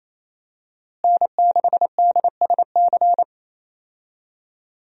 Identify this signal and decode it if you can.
Morse code, heard as a keyed tone, about 28 wpm, 700 Hz. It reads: N6DSC
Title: USA Call Signs - 5 Characters - Repeated 2x - 28wpm